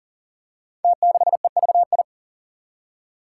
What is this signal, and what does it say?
Morse code, heard as a keyed tone, about 43 wpm, 700 Hz. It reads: T6EVI
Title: Five Challenging Letter-Number Combinations 40wpm